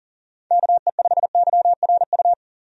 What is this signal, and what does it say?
Morse code, heard as a keyed tone, about 40 wpm, 700 Hz. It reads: KE5YRU
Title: USA Call Signs - 6 Characters - 40wpm